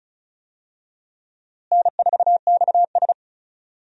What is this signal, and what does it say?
Morse code, heard as a keyed tone, about 36 wpm, 700 Hz. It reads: N4XS
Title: USA Call Signs - 4 Characters - Repeated 2x - 35wpm